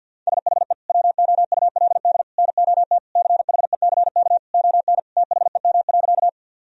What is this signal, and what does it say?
Morse code, heard as a keyed tone, about 51 wpm, 700 Hz. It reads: SHE WOULD NOT CHECK ON THEM.
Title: Sentences from Top 1000 Words - Review Part 3 - 50wpm